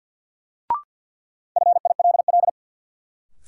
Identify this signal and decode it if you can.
Morse code, heard as a keyed tone, about 53 wpm, 700 Hz. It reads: FILL
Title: Sentences from Top 1000 Words - Review Part 3 - New Word Drill - 50wpm